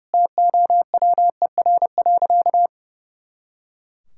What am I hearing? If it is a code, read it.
Morse code, heard as a keyed tone, about 30 wpm, 700 Hz. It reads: TOWER.
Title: Pangram Challenge v2 - 30wpm